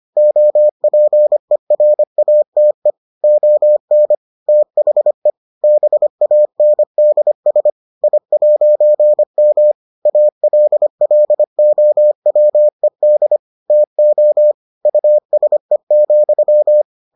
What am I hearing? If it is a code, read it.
Morse code, heard as a keyed tone, about 25 wpm, 600 Hz. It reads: OPERATE ON THE BANDS I'M ALLOWED TO USE,